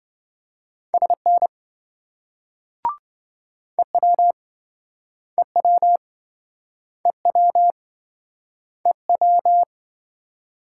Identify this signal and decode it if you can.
Morse code, heard as a keyed tone, about 27 wpm, 700 Hz. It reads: SN EW EW EW EW